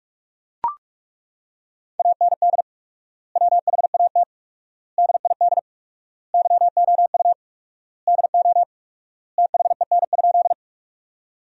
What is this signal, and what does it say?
Morse code, heard as a keyed tone, about 41 wpm, 700 Hz. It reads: AND WHAT DID YOU DO THEN?
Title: 4x Word-Spacing - Sentences from Top 100 Words - 45wpm